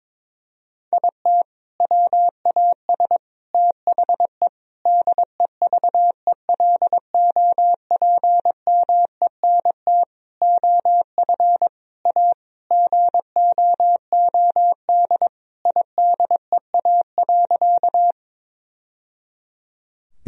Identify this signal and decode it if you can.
Morse code, heard as a keyed tone, about 22 wpm, 700 Hz. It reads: IT WAS THE DEVELOPMENT OF A GOOD IDEA.
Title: Sentences from Top 1000 Words - Review Part 4 - 22wpm